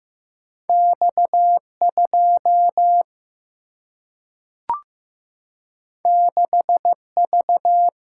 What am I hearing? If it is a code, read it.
Morse code, heard as a keyed tone, about 15 wpm, 700 Hz. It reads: X2 6V
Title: Single-Letter Single-Number 15wpm